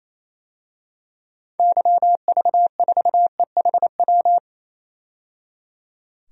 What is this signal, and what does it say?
Morse code, heard as a keyed tone, about 28 wpm, 700 Hz. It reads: YV4EHW
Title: DXCC-100 Call Signs - 6 Characters - 28wpm